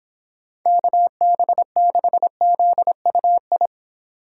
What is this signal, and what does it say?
Morse code, heard as a keyed tone, about 25 wpm, 700 Hz. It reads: KB6ZUI